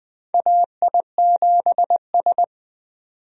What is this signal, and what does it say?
Morse code, heard as a keyed tone, about 20 wpm, 700 Hz. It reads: AI7S
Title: USA Call Signs - 4 Characters - 20wpm